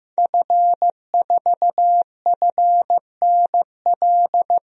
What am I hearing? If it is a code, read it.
Morse code, heard as a keyed tone, about 15 wpm, 700 Hz. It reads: F4FNL
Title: DXCC-100 Call Signs - 5 Characters - Repeated 3x - 15wpm